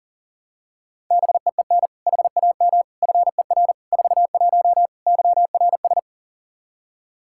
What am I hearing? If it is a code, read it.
Morse code, heard as a keyed tone, about 40 wpm, 700 Hz. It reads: BEEN HAM FER 41 YRS
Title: Practice QSO - Exchange 3 - More Info - 40wpm